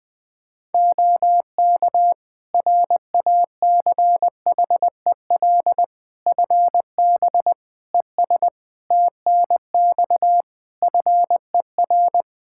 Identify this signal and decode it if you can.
Morse code, heard as a keyed tone, about 20 wpm, 700 Hz. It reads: OK RACHEL FB ES TNX FER
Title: Practice QSO - Exchange 2 - Info - 20wpm